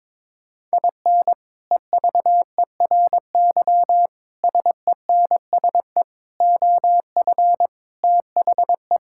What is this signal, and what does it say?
Morse code, heard as a keyed tone, about 22 wpm, 700 Hz. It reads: IN EVERY SENSE OF THE
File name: practice / 22wpm / Sets of 6 Words - 22wpm